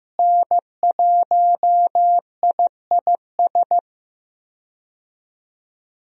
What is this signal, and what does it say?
Morse code, heard as a keyed tone, about 15 wpm, 700 Hz. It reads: N1IIS
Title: USA Call Signs - 5 Characters - Repeated 3x - 15wpm